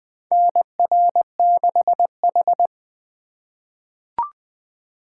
Morse code, heard as a keyed tone, about 20 wpm, 700 Hz. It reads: NR6H
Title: USA Call Signs - 4 Characters - 2x Character Spacing - 20wpm